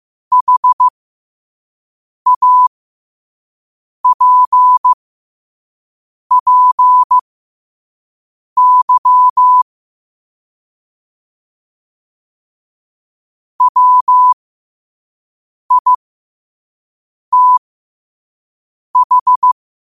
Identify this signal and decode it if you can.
Morse code, heard as a keyed tone, about 15 wpm, spaced as for 5 wpm, 1000 Hz. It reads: HAPPY WITH
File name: qotdfiles / 5wpm